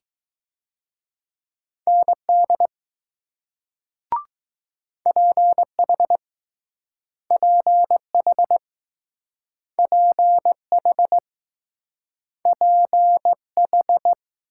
Morse code, heard as a keyed tone, about 19 wpm, 700 Hz. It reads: ND PH PH PH PH